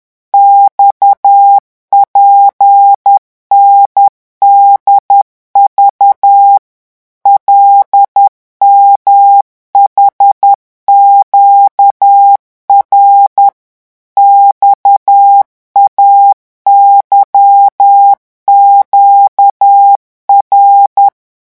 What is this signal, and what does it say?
Morse code, heard as a keyed tone, about 10 wpm, 800 Hz. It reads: XPNDV LMHQR XAYQR
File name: morse mp3 / revision5-vitesse 10 mots